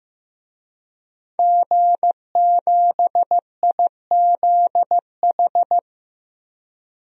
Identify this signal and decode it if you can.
Morse code, heard as a keyed tone, about 14 wpm, 700 Hz. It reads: G7IZH
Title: DXCC-100 Call Signs - 5 Characters - Repeated 2x - 15wpm